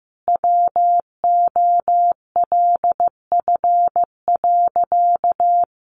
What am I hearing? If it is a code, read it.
Morse code, heard as a keyed tone, about 15 wpm, 700 Hz. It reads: WOLF.
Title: Additional Easy Phrases 15wpm